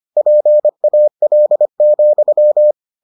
Morse code, heard as a keyed tone, about 25 wpm, 600 Hz. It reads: PAL,